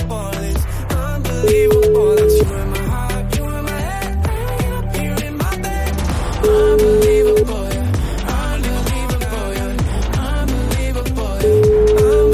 Ringback tone (Europe, 425 Hz), with noise.